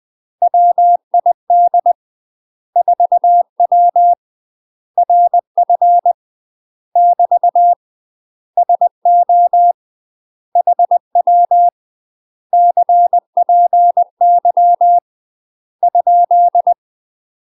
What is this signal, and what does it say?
Morse code, heard as a keyed tone, about 20 wpm, 700 Hz. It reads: WID 4W RF <BT> SO HW CPY ?
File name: UK MASTER QSOs TRAINING/20WPM